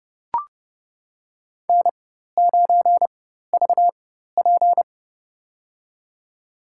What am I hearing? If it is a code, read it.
Morse code, heard as a keyed tone, about 31 wpm, spaced as for 14 wpm, 700 Hz. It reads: N9VP
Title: USA Call Signs - 4 Characters - 4x Character Spacing - 30wpm